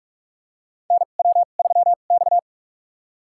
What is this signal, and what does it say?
Morse code, heard as a keyed tone, about 45 wpm, spaced as for 32 wpm, 700 Hz. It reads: NW3X
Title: USA Call Signs - 2x Character Spacing - 45wpm